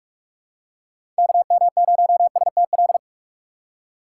Morse code, heard as a keyed tone, about 45 wpm, 700 Hz. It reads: KM0STL
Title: USA Call Signs - 4x Character Spacing - 45wpm